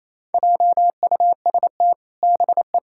Morse code, heard as a keyed tone, about 28 wpm, 700 Hz. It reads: JUST BE
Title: Sentences from Top 1600 Words 28wpm